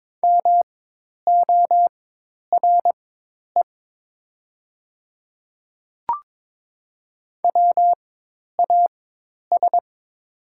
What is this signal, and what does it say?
Morse code, heard as a keyed tone, about 22 wpm, spaced as for 10 wpm, 700 Hz. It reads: MORE WAS
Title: 4x Character Spacing - Top 300 Words - 22wpm